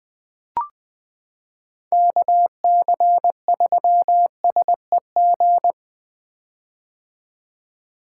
Morse code, heard as a keyed tone, about 20 wpm, 700 Hz. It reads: KC3SEG
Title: 2023 POTA Activator Call Signs - 20wpm